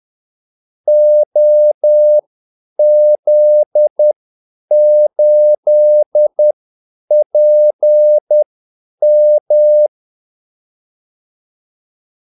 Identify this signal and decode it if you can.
Morse code, heard as a keyed tone, about 10 wpm, spaced as for 8 wpm, 600 Hz. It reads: OZ8PM
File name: Callsigns-10-8wpm